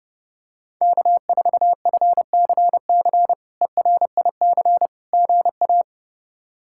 Morse code, heard as a keyed tone, about 32 wpm, 700 Hz. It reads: K4FCC ERIC GA
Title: K1USN SST QSO Practice - 30wpm